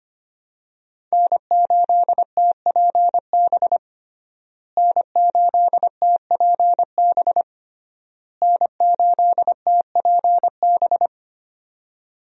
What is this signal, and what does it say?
Morse code, heard as a keyed tone, about 25 wpm, 700 Hz. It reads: N8TPB N8TPB N8TPB
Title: USA Call Signs - 5 Characters - Repeated 3x - 25wpm